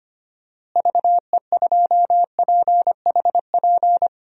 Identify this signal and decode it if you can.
Morse code, heard as a keyed tone, about 25 wpm, 700 Hz. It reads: VE2PHP
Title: DXCC-100 Call Signs - 6 Characters - Repeated 2x - 25wpm